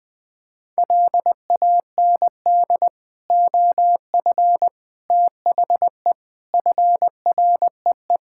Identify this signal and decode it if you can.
Morse code, heard as a keyed tone, about 21 wpm, 700 Hz. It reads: LAND OF THE FREE
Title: Sentences with 9-Word Phrases - 20wpm